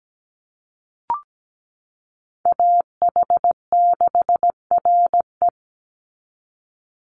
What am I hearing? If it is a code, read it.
Morse code, heard as a keyed tone, about 17 wpm, 700 Hz. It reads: AH6RE
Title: Call Signs Repeated 3x - 17wpm